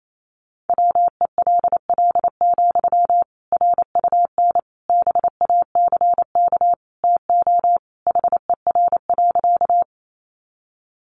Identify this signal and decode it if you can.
Morse code, heard as a keyed tone, about 28 wpm, 700 Hz. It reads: WELL, RUN BACK TO HER.
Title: Sentences from Top 300 Words 28wpm